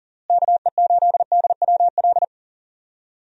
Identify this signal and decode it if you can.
Morse code, heard as a keyed tone, about 40 wpm, 700 Hz. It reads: KE8DWL
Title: USA Call Signs - 6 Characters - Repeated 3x - 40wpm